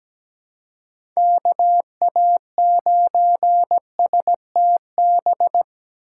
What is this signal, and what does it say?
Morse code, heard as a keyed tone, about 17 wpm, 700 Hz. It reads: KA9STB
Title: USA Call Signs - 6 Characters - 2x Character Spacing - 17wpm